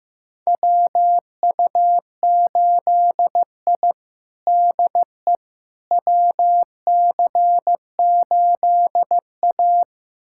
Morse code, heard as a keyed tone, about 15 wpm, 700 Hz. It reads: WU8I DE WC8A
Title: USA Callsign DE Callsign - 4 Characters - 15wpm